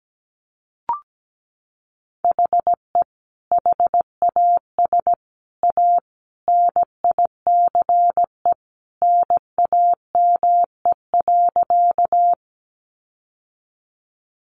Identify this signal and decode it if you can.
Morse code, heard as a keyed tone, about 17 wpm, 700 Hz. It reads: HE HAS A NICE NAME.
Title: Sentences from Top 1100 Words 17wpm